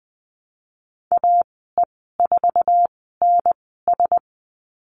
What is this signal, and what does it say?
Morse code, heard as a keyed tone, about 20 wpm, spaced as for 15 wpm, 700 Hz. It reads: AE4NS
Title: 055 Lesson 10g - Letter r - Call Signs 20wpm